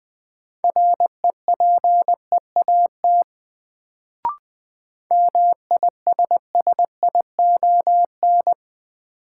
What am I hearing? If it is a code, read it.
Morse code, heard as a keyed tone, about 20 wpm, 700 Hz. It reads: REPEAT MISSION
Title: Sentences from Top 1600 Words - New Word Drill Alt - 20wpm